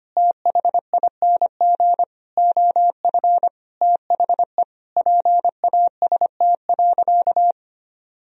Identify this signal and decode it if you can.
Morse code, heard as a keyed tone, about 25 wpm, 700 Hz. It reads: THING OF THE PAST.
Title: Sentences from Top 1000 Words - Review Part 3 - 25wpm